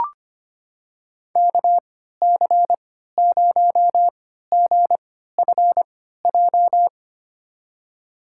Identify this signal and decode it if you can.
Morse code, heard as a keyed tone, about 25 wpm, spaced as for 14 wpm, 700 Hz. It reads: KC0GFJ
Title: USA Call Signs - 6 Characters - 3x Character Spacing - 25wpm